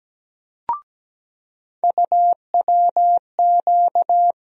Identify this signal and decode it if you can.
Morse code, heard as a keyed tone, about 17 wpm, 700 Hz. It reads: UWQ
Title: USA Call Sign Postfixes - 3 Letters 17wpm